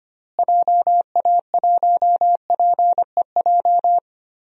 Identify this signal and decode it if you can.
Morse code, heard as a keyed tone, about 25 wpm, 700 Hz. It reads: JA1PEJ
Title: DXCC-100 Call Signs - 6 Characters - Repeated 3x - 25wpm